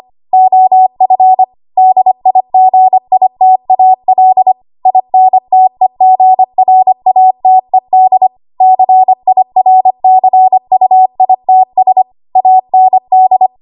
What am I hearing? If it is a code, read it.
Morse code, heard as a keyed tone, about 25 wpm, 750 Hz. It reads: OF DIGITAL INTEGRATED CIRCUITS AND
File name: Archive / 25 WPM / 25WPM